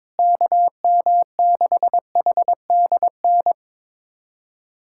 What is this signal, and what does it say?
Morse code, heard as a keyed tone, about 21 wpm, 700 Hz. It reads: KM6HDN
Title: USA Callsign DE Callsign - 6 Characters - 22wpm